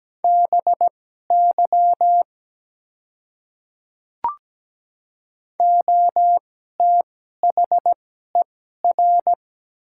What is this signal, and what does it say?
Morse code, heard as a keyed tone, about 17 wpm, spaced as for 12 wpm, 700 Hz. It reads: BY OTHER
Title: 2x Character Spacing - Top 300 Words - 17wpm